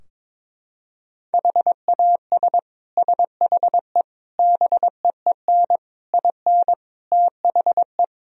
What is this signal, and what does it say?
Morse code, heard as a keyed tone, about 22 wpm, 700 Hz. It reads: HAS SHE BEEN IN THE
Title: Sentences from Top 1000 Words - Review Part 4 - 22wpm